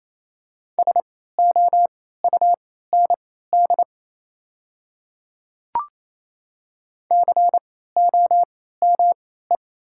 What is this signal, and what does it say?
Morse code, heard as a keyed tone, about 27 wpm, spaced as for 16 wpm, 690 Hz. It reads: SOUND COME
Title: 3x Character Spacing - Top 100 Words - 28wpm